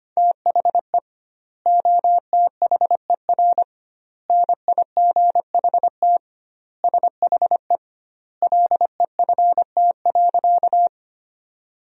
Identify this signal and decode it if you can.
Morse code, heard as a keyed tone, about 25 wpm, 700 Hz. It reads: THE OTHER NIGHT SHE LEFT.
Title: 2x Word-Spacing - Sentences from Top 300 Words - 25wpm